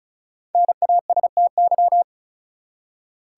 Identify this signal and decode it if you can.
Morse code, heard as a keyed tone, about 36 wpm, 700 Hz. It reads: NASTY
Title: Five Letter Words - 35wpm